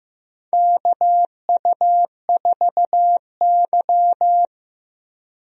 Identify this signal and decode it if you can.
Morse code, heard as a keyed tone, about 15 wpm, 700 Hz. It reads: KU4Y
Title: USA Call Signs - 4 Characters - Repeated 2x - 15wpm